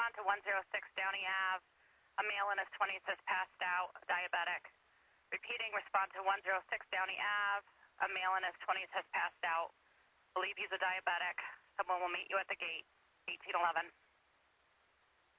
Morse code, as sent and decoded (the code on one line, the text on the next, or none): none